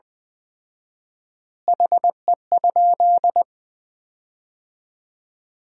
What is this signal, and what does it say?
Morse code, heard as a keyed tone, about 20 wpm, 700 Hz. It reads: HE?
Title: 4x Word-Spacing - Sentences from Top 300 Words - 20wpm